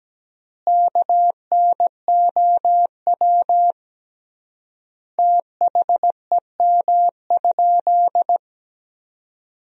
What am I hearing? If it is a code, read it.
Morse code, heard as a keyed tone, about 17 wpm, 700 Hz. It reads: KNOW THEM?
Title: 3x Word-Spacing - Sentences from Top 100 Words - 17wpm